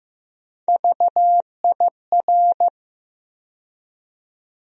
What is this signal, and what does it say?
Morse code, heard as a keyed tone, about 15 wpm, 700 Hz. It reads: VIR
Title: USA Call Sign Postfixes - 3 Letters 15wpm